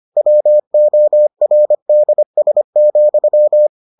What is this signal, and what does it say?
Morse code, heard as a keyed tone, about 25 wpm, 600 Hz. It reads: WORDS,